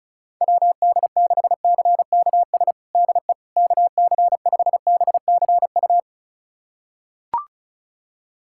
Morse code, heard as a keyed tone, about 36 wpm, 700 Hz. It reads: WD6CKS DE KC5BCU
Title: USA Callsign DE Callsign - 6 Characters - 35wpm